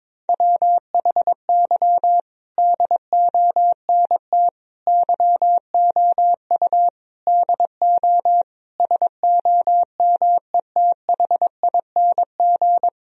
Morse code, heard as a keyed tone, about 22 wpm, 700 Hz. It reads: WHY DONT YOU DO SOMETHING
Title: Sentences from Top 1000 Words - Review Part 4 - 22wpm